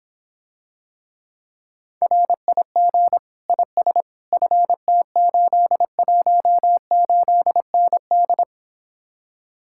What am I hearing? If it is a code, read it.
Morse code, heard as a keyed tone, about 26 wpm, 700 Hz. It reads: RIG IS FT818ND